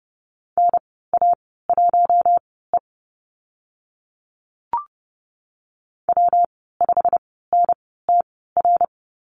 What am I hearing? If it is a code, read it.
Morse code, heard as a keyed tone, about 30 wpm, spaced as for 17 wpm, 700 Hz. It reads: NA1E W5NTR
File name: Lesson 19g - Letter w - Call Signs 30wpm